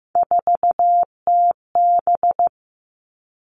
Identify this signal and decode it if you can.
Morse code, heard as a keyed tone, about 15 wpm, 700 Hz. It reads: 4TB
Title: All Cut Numbers - 3-Digit 15wpm